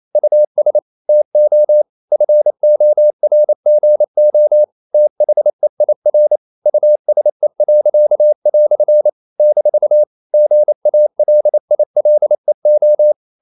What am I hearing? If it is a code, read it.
Morse code, heard as a keyed tone, about 28 wpm, 600 Hz. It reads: US TO FORGO THEIR USE." - GALILEO
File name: CW / Practice / EnglishQuotations0128